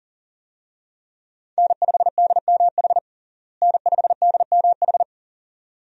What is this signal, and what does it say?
Morse code, heard as a keyed tone, about 37 wpm, 700 Hz. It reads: N5DMH N5DMH
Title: USA Call Signs - 5 Characters - Repeated 2x - 40wpm